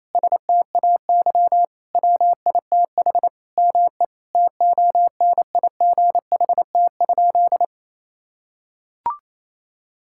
Morse code, heard as a keyed tone, about 28 wpm, 700 Hz. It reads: STAY WITH ME TONIGHT?
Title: Sentences from Top 1600 Words 28wpm